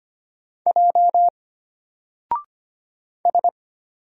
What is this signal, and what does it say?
Morse code, heard as a keyed tone, about 25 wpm, 700 Hz. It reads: J S
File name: Single-Letter-Number-25wpm